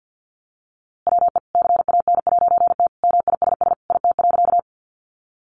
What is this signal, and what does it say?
Morse code, heard as a keyed tone, about 50 wpm, 700 Hz. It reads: WE CAN'T MISS IT.
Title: A Sound Of Thunder 50wpm